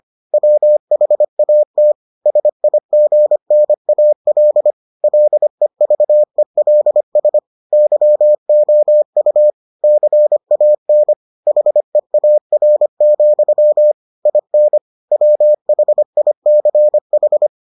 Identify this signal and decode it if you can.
Morse code, heard as a keyed tone, about 25 wpm, 600 Hz. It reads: WHAT SIGNAL LEVELS YOU CAN HEAR, IN WHICH